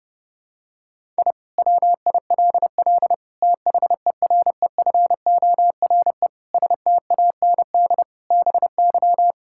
Morse code, heard as a keyed tone, about 29 wpm, 700 Hz. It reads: I WILL THEREFORE STAND BY
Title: Sentences from Top 1000 Words - Review Part 5 - 30wpm